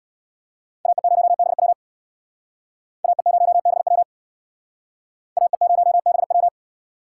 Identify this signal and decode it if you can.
Morse code, heard as a keyed tone, about 61 wpm, 700 Hz. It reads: AE0BW AE0BW AE0BW
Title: Speed Racing - USA Call Signs 50wpm